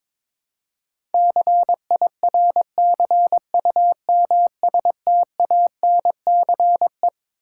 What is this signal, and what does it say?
Morse code, heard as a keyed tone, about 22 wpm, 700 Hz. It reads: CIRCUMSTANCE
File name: Twelve Letter Words - 22wpm